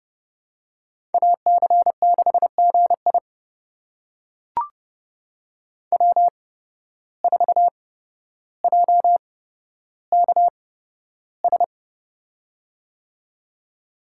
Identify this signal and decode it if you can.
Morse code, heard as a keyed tone, about 30 wpm, spaced as for 13 wpm, 700 Hz. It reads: AC6GI W4JKS
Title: USA Call Signs - 5 Characters - 8x Character Spacing - 30wpm